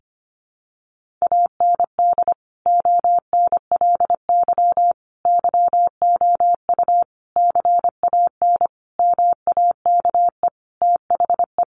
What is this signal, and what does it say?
Morse code, heard as a keyed tone, about 25 wpm, 700 Hz. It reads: AND ONLY YOU CAN MAKE THE
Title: Sentences from Top 1400 Words 25wpm